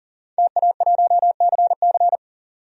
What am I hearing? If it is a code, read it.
Morse code, heard as a keyed tone, about 40 wpm, 700 Hz. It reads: TA1CC
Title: DXCC-100 Call Signs - 5 Characters - 40wpm